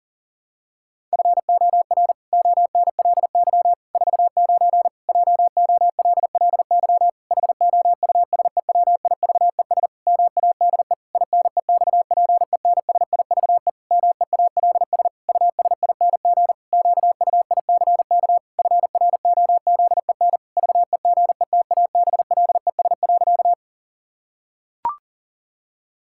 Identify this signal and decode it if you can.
Morse code, heard as a keyed tone, about 40 wpm, 700 Hz. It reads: FOR ONLY 49 JOLLY HOUSEWIVES MADE INEXPENSIVE MEALS USING QUICK FROZEN VEGETABLES.
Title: Pangram Challenge v2 - 40wpm